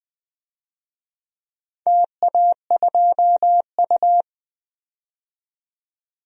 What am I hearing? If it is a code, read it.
Morse code, heard as a keyed tone, about 20 wpm, 700 Hz. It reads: TA2U
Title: DXCC-100 Call Signs - 4 Characters - 20wpm